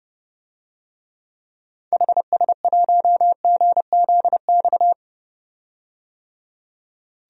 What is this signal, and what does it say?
Morse code, heard as a keyed tone, about 30 wpm, 700 Hz. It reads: HS1GZX
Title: DXCC-100 Call Signs - 6 Characters - 30wpm